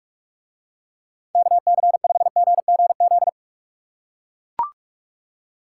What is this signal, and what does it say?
Morse code, heard as a keyed tone, about 44 wpm, 700 Hz. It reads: KC5GGZ